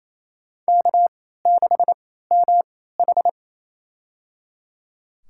Morse code, heard as a keyed tone, about 28 wpm, spaced as for 16 wpm, 700 Hz. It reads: K6MH
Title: USA Call Signs - 4 Characters - 3x Character Spacing - 28wpm